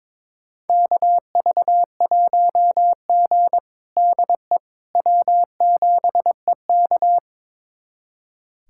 Morse code, heard as a keyed tone, about 22 wpm, 700 Hz. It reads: KV1G DE W7EK
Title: USA Callsign DE Callsign - 4 Characters - 22wpm